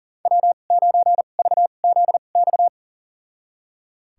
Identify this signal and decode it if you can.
Morse code, heard as a keyed tone, about 37 wpm, spaced as for 28 wpm, 700 Hz. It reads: W9VZX
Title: USA Call Signs - 5 Characters - 2x Character Spacing - 40wpm